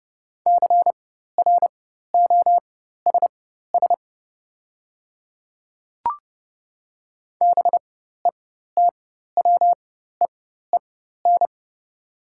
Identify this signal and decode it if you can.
Morse code, heard as a keyed tone, about 30 wpm, spaced as for 14 wpm, 700 Hz. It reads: CROSS BETWEEN
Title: 4x Character Spacing - Top 300 Words - 30wpm